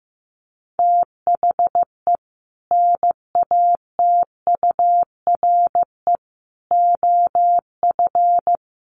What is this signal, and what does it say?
Morse code, heard as a keyed tone, about 15 wpm, 700 Hz. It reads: THE NATURE OF
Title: Sets of 4 Words 15wpm